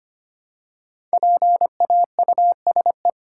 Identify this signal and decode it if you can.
Morse code, heard as a keyed tone, about 25 wpm, 700 Hz. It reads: PAUSE
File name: 5 Letter Words - First 3 Letters Given - 25wpm